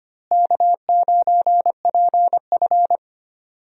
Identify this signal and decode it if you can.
Morse code, heard as a keyed tone, about 25 wpm, 700 Hz. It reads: K9PF
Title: USA Call Signs - 4 Characters - Repeated 3x - 25wpm